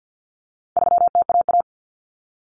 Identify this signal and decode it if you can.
Morse code, heard as a keyed tone, about 50 wpm, 700 Hz. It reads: 3TAA
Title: Common Cut Numbers - 4-Digit 50wpm